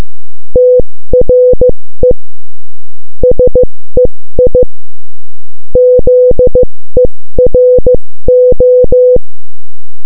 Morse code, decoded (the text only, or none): TRE SEI ZERO